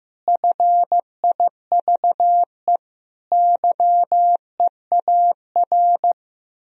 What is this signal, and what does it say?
Morse code, heard as a keyed tone, about 15 wpm, 700 Hz. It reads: FIVE YEAR
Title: Sentences from Top 1000 Words - Review Part 2 - 15wpm